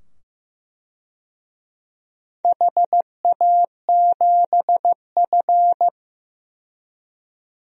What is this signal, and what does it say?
Morse code, heard as a keyed tone, about 15 wpm, 700 Hz. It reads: HA7F
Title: DXCC-100 Call Signs - 4 Characters - 15wpm